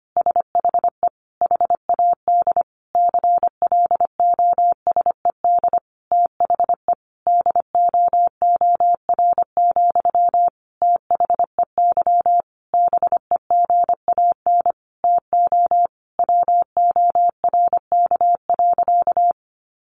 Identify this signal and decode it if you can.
Morse code, heard as a keyed tone, about 25 wpm, 700 Hz. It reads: SHE HAD CLOSED THE DOOR, THEY BEGAN TO WORK.
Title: Sentences from Top 1400 Words 25wpm